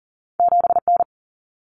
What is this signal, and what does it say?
Morse code, heard as a keyed tone, about 40 wpm, 700 Hz. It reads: 7N
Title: Common Cut Numbers - 2-Digit 40wpm